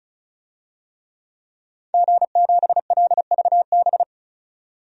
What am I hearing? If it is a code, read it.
Morse code, heard as a keyed tone, about 34 wpm, 700 Hz. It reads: G7LVB
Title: DXCC-100 Call Signs - 5 Characters - 35wpm